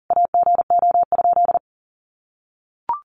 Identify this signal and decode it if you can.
Morse code, heard as a keyed tone, about 40 wpm, 700 Hz. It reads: AGO?
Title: Sentences from Top 1300 Words 40wpm